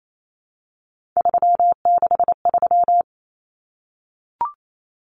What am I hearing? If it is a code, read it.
Morse code, heard as a keyed tone, about 28 wpm, 700 Hz. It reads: 363
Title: Common Cut Numbers - 3-Digit 28wpm